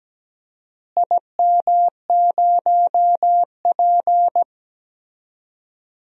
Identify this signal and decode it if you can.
Morse code, heard as a keyed tone, about 17 wpm, 700 Hz. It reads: IM0P
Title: DXCC-100 Call Signs - 4 Characters - Repeated 3x - 17wpm